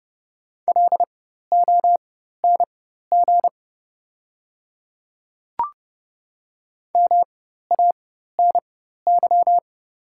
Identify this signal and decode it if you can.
Morse code, heard as a keyed tone, about 30 wpm, spaced as for 14 wpm, 700 Hz. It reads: LONG MANY
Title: 4x Character Spacing - Top 100 Words - 30wpm